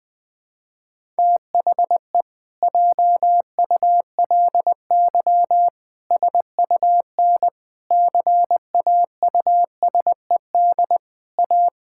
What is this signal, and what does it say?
Morse code, heard as a keyed tone, about 20 wpm, 700 Hz. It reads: THE JULY SUN CAUSED A
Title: Pangram Challenge v2 - 20wpm